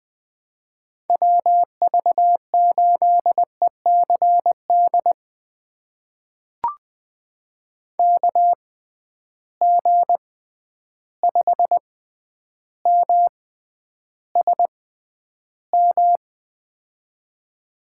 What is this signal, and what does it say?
Morse code, heard as a keyed tone, about 21 wpm, spaced as for 10 wpm, 700 Hz. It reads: WV8ECD KG5MSM
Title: USA Call Signs - 6 Characters - 6x Character Spacing - 20wpm